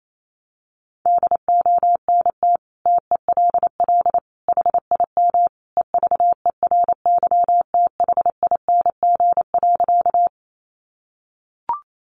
Morse code, heard as a keyed tone, about 28 wpm, 700 Hz. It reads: DONT TELL HIM EVERYTHING.
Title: Sentences from Top 700 Words 28wpm